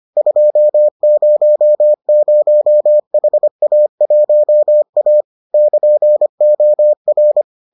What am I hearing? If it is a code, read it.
Morse code, heard as a keyed tone, about 25 wpm, 600 Hz. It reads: 200HA1A <KN>OR